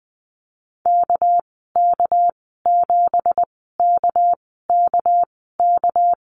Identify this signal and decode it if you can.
Morse code, heard as a keyed tone, about 20 wpm, spaced as for 14 wpm, 700 Hz. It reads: KK7KKK
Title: 213 Lesson 34g - Number 8 - Call Signs 20wpm